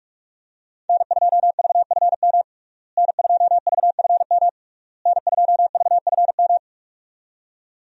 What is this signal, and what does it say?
Morse code, heard as a keyed tone, about 41 wpm, 700 Hz. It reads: N2VFM N2VFM N2VFM
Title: USA Call Signs - 5 Characters - Repeated 3x - 45wpm